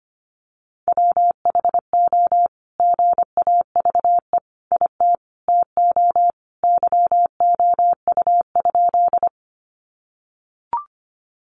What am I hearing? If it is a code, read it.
Morse code, heard as a keyed tone, about 25 wpm, 700 Hz. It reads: WHO GAVE IT TO YOU?
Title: Sentences from Top 500 Words 25wpm